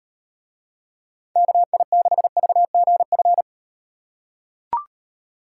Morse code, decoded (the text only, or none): KI6VGF